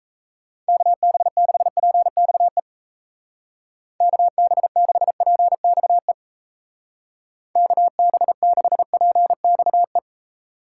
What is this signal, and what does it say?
Morse code, heard as a keyed tone, about 34 wpm, 700 Hz. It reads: KB6PXE KB6PXE KB6PXE
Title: Speed Racing - USA Call Signs 28wpm